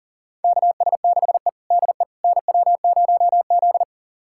Morse code, heard as a keyed tone, about 40 wpm, 690 Hz. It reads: KS6E DE NW0Z
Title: USA Callsign DE Callsign - 4 Characters - 40wpm